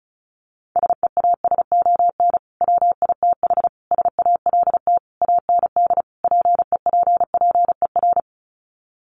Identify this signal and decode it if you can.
Morse code, heard as a keyed tone, about 34 wpm, 700 Hz. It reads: SEASON WITH SALT AND PEPPER
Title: Sets of 5 Words 35wpm